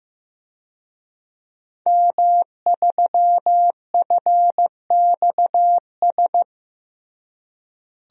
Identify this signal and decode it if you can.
Morse code, heard as a keyed tone, about 15 wpm, 700 Hz. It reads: M3FXS
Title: DXCC-100 Call Signs - 5 Characters - Repeated 2x - 15wpm